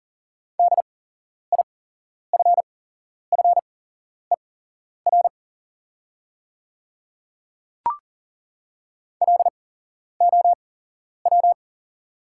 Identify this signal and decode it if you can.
Morse code, heard as a keyed tone, about 40 wpm, spaced as for 11 wpm, 700 Hz. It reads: DIFFER LOW
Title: 8x Character Spacing - Top 200 Words - 40wpm